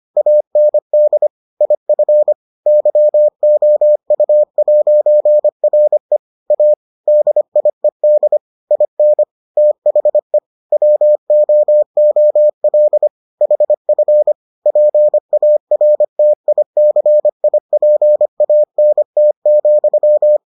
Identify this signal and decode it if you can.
Morse code, heard as a keyed tone, about 25 wpm, 600 Hz. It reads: AND IF YOU'RE A DIED IN THE WOOL HF PARTICIPANT,